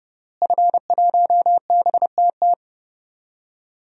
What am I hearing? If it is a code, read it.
Morse code, heard as a keyed tone, about 29 wpm, 700 Hz. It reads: F1BTT
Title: DXCC-100 Call Signs - 5 Characters - Repeated 3x - 30wpm